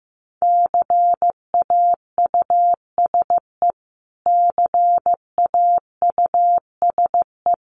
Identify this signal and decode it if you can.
Morse code, heard as a keyed tone, about 15 wpm, 700 Hz. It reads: CAUSE CAUSE
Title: Top 200 Words Repeated 15wpm